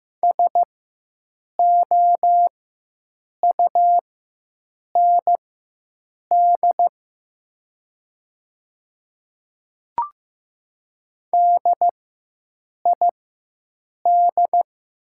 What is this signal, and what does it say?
Morse code, heard as a keyed tone, about 15 wpm, spaced as for 7 wpm, 700 Hz. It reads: SOUND DID
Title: 4x Character Spacing - Top 100 Words - 15wpm